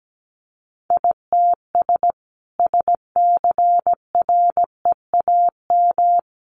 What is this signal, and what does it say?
Morse code, heard as a keyed tone, about 17 wpm, 700 Hz. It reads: ITS SCREAM
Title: A Sound Of Thunder 17wpm